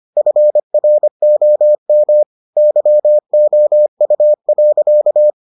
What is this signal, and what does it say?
Morse code, heard as a keyed tone, about 25 wpm, 600 Hz. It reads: FROM YOU.